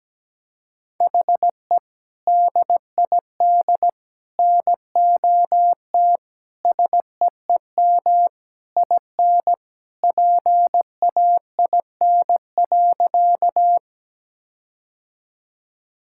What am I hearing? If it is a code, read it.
Morse code, heard as a keyed tone, about 17 wpm, 700 Hz. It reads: HE DID NOT SEEM IN PAIN.
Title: Sentences from Top 1600 Words 17wpm